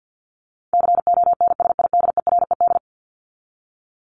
Morse code, heard as a keyed tone, about 50 wpm, 700 Hz. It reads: CONSIDERED
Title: ED Suffix - Top 100 Words with Base Words 50wpm